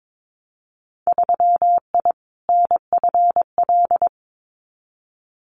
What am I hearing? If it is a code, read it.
Morse code, heard as a keyed tone, about 22 wpm, 700 Hz. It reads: 3I NFL
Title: Winter Field Day Exchanges 22wpm